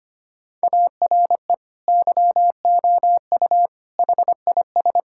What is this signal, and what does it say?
Morse code, heard as a keyed tone, about 25 wpm, 700 Hz. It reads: ARE YOU HIS
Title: Sentences from Top 1600 Words 25wpm